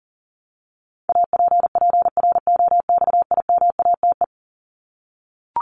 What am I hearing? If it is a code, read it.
Morse code, heard as a keyed tone, about 37 wpm, 700 Hz. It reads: APPROXIMATE
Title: LY Suffix - Top 100 Words with Base Words 40wpm